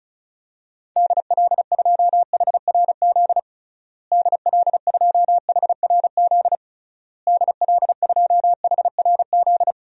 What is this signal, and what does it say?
Morse code, heard as a keyed tone, about 36 wpm, 700 Hz. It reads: DL2HRZ DL2HRZ DL2HRZ
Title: DXCC-100 Call Signs - 6 Characters - Repeated 3x - 35wpm